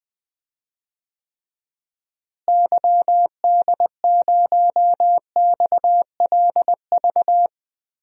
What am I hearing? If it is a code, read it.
Morse code, heard as a keyed tone, about 20 wpm, 700 Hz. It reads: YD0XLV
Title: DXCC-100 Call Signs - 6 Characters - Repeated 3x - 20wpm